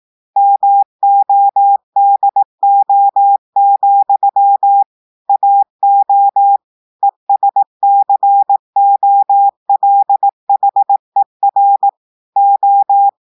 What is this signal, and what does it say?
Morse code, heard as a keyed tone, about 18 wpm, 800 Hz. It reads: MODO, AO ESCOLHER O
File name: audios / portugais / portugais texte en clair 2 vitesse 18 mots minutes